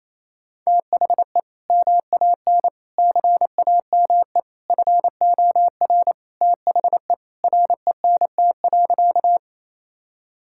Morse code, heard as a keyed tone, about 28 wpm, 700 Hz. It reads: THE MAN CAME FOR THE RENT.
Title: Sentences from Top 1000 Words - Review Part 5 - 28wpm